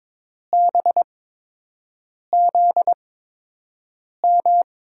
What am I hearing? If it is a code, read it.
Morse code, heard as a keyed tone, about 22 wpm, spaced as for 6 wpm, 700 Hz. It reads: BZM
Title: USA Call Signs - 6 Characters - 8x Character Spacing - 22wpm